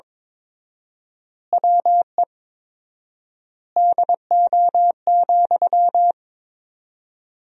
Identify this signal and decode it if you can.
Morse code, heard as a keyed tone, about 22 wpm, 700 Hz. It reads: WE DO,
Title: 4x Word-Spacing - Sentences from Top 100 Words - 22wpm